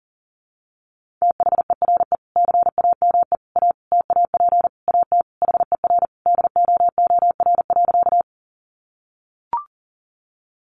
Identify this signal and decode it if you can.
Morse code, heard as a keyed tone, about 40 wpm, 700 Hz. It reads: THERE CAME A TAP AT HER DOOR.
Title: Sentences from Top 1400 Words 40wpm